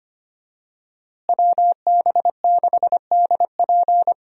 Morse code, heard as a keyed tone, about 25 wpm, 700 Hz. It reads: WB6DP
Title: USA Call Signs - 5 Characters - Repeated 2x - 25wpm